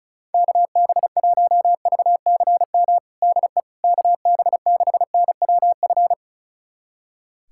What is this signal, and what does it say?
Morse code, heard as a keyed tone, about 36 wpm, 700 Hz. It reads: KB1VCM DE KB6NWF
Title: USA Callsign DE Callsign - 6 Characters - 35wpm